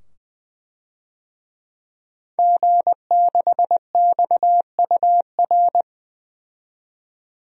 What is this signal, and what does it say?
Morse code, heard as a keyed tone, about 20 wpm, 700 Hz. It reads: G6XUR
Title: DXCC-100 Call Signs - 5 Characters - 20wpm